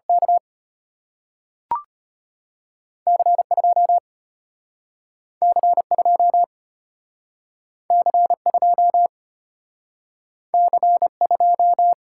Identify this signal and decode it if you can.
Morse code, heard as a keyed tone, about 33 wpm, 700 Hz. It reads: K C 2 C 2 C 2 C 2